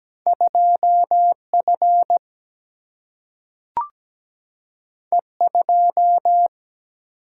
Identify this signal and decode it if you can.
Morse code, heard as a keyed tone, about 17 wpm, 700 Hz. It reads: 2F E2
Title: Single-Letter Single-Number 17wpm